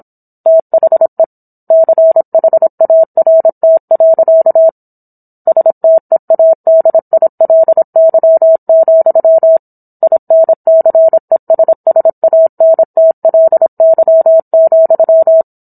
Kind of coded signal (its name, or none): Morse code